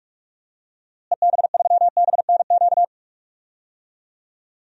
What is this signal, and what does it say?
Morse code, heard as a keyed tone, about 47 wpm, 700 Hz. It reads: EB3BNQ
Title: DXCC-100 Call Signs - 6 Characters - Repeated 2x - 45wpm